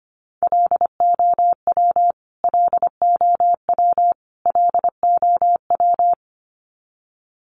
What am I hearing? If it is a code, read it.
Morse code, heard as a keyed tone, about 25 wpm, 700 Hz. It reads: LOW LOW LOW
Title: Top 500 Words Repeated 25wpm